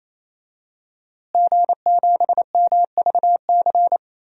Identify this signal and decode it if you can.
Morse code, heard as a keyed tone, about 27 wpm, 700 Hz. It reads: G7MVC
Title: DXCC-100 Call Signs - 5 Characters - Repeated 3x - 28wpm